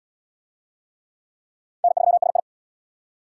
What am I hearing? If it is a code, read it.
Morse code, heard as a keyed tone, about 73 wpm, 700 Hz. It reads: N3SI